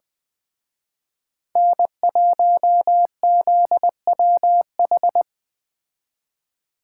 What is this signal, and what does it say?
Morse code, heard as a keyed tone, about 20 wpm, 700 Hz. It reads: N1ZWH
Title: Speed Racing - USA Call Signs 17wpm